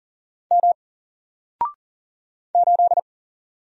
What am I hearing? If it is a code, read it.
Morse code, heard as a keyed tone, about 40 wpm, 700 Hz. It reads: M 8
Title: Single Letter-Number Alt 40wpm